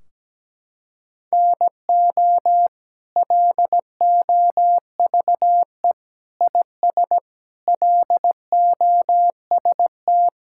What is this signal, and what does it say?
Morse code, heard as a keyed tone, about 17 wpm, 700 Hz. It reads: NO LOVE IS LOST
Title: Sentences from Top 1600 Words 17wpm